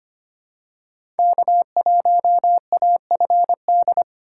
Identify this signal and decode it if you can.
Morse code, heard as a keyed tone, about 25 wpm, 700 Hz. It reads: K1AFD
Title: USA Call Signs - 5 Characters - 3x Character Spacing - 25wpm